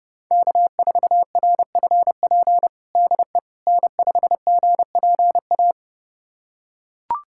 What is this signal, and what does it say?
Morse code, heard as a keyed tone, about 32 wpm, 700 Hz. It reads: K4RFP DE N5GPA
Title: USA Callsign DE Callsign - 5 Characters - 30wpm